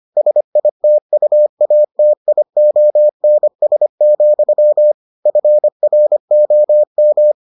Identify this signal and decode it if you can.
Morse code, heard as a keyed tone, about 25 wpm, 600 Hz. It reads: SITUATIONS, FROM